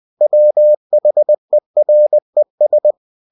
Morse code, heard as a keyed tone, about 20 wpm, 600 Hz. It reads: WHERES
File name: cw-books / winnie-the-pooh / winnie-20wpm-page-0069